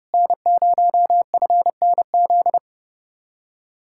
Morse code, heard as a keyed tone, about 36 wpm, 700 Hz. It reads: N0FNZ
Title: Speed-Racing Common QSO Phrases - 30wpm